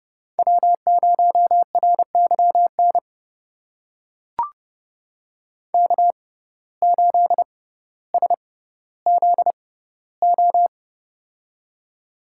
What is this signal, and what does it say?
Morse code, heard as a keyed tone, about 31 wpm, spaced as for 16 wpm, 700 Hz. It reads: W0RYN K8SZO
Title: USA Call Signs - 5 Characters - 6x Character Spacing - 30wpm